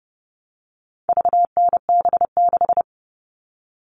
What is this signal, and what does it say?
Morse code, heard as a keyed tone, about 30 wpm, 700 Hz. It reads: VNB6
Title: All Cut Numbers - 4-Digit 30wpm